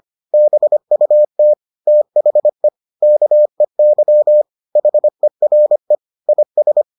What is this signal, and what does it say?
Morse code, heard as a keyed tone, about 25 wpm, 600 Hz. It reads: BUT THE KEY HERE IS